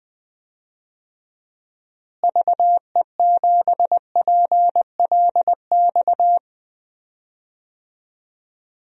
Morse code, heard as a keyed tone, about 20 wpm, 700 Hz. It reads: VE7PLX
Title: DXCC-100 Call Signs - 6 Characters - Repeated 2x - 20wpm